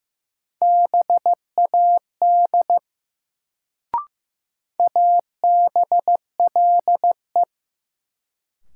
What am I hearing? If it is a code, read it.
Morse code, heard as a keyed tone, about 15 wpm, 700 Hz. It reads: BAD ABLE
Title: Sentences from Top 700 Words - New Word Drill - 15wpm